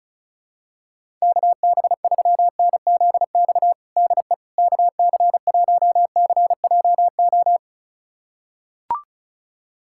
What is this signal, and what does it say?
Morse code, heard as a keyed tone, about 34 wpm, 700 Hz. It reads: KB3NZX DE KC1CJO
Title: USA Callsign DE Callsign 35wpm